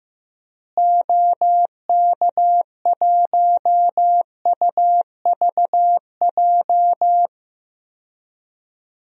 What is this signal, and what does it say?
Morse code, heard as a keyed tone, about 15 wpm, 700 Hz. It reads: OK1UVJ
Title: DXCC-100 Call Signs - 6 Characters - Repeated 2x - 15wpm